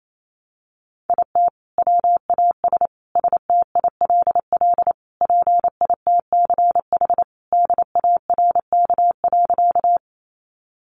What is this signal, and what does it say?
Morse code, heard as a keyed tone, about 28 wpm, 700 Hz. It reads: IT WAS STILL PITCH DARK.
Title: Sentences from Top 1300 Words 28wpm